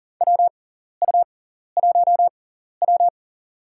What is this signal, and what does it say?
Morse code, heard as a keyed tone, about 40 wpm, spaced as for 14 wpm, 700 Hz. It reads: WU1W
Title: USA Call Signs - 4 Characters - 6x Character Spacing - 40wpm